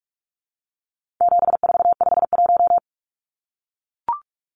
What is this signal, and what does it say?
Morse code, heard as a keyed tone, about 44 wpm, 700 Hz. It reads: 7451
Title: Numbers - 4-Digit 45wpm